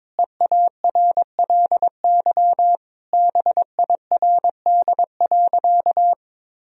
Morse code, heard as a keyed tone, about 21 wpm, 700 Hz. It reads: EARLY BIRD.
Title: Sentences from Top 1000 Words - Review Part 2 - 22wpm